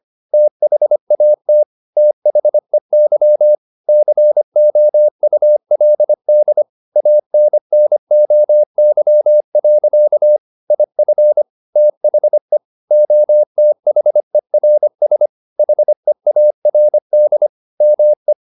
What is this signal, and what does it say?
Morse code, heard as a keyed tone, about 25 wpm, 600 Hz. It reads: THAT THEY COULD ANNOY. IF THE OTHERS HEARD ME
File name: the old man and the sea part 1 25wpm